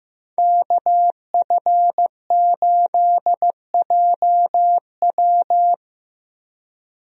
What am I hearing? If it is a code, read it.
Morse code, heard as a keyed tone, about 15 wpm, 700 Hz. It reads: KF8JW
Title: USA Call Signs - 5 Characters - Repeated 2x - 15wpm